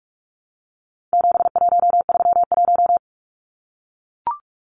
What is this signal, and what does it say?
Morse code, heard as a keyed tone, about 44 wpm, 700 Hz. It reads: 7131
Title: Numbers - 4-Digit 45wpm